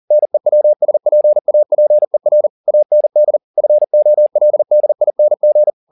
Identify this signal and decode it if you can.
Morse code, heard as a keyed tone, about 40 wpm, 600 Hz. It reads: NEWSPAPER AND FOLDING